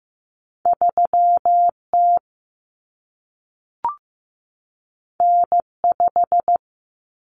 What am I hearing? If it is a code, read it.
Morse code, heard as a keyed tone, about 15 wpm, 700 Hz. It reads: 3T N5
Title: Common Cut Numbers - 2-Digit 15wpm